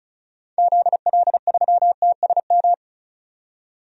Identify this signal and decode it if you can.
Morse code, heard as a keyed tone, about 36 wpm, 700 Hz. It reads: ZL3TSM
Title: DXCC-100 Call Signs - 6 Characters - 35wpm